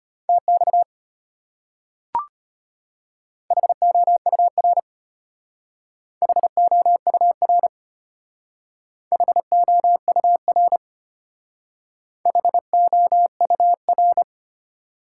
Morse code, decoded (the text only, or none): TX HOUR HOUR HOUR HOUR